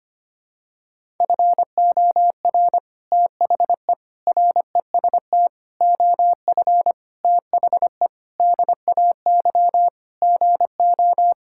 Morse code, heard as a keyed tone, about 25 wpm, 700 Hz. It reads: FOR THE REST OF THE DAY GO
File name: practice / 25wpm / Sentences with 3-Word Phrases - 25wpm